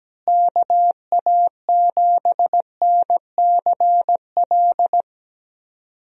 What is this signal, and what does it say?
Morse code, heard as a keyed tone, about 17 wpm, 700 Hz. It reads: KA7NCL
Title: USA Call Signs - 6 Characters - Repeated 3x - 17wpm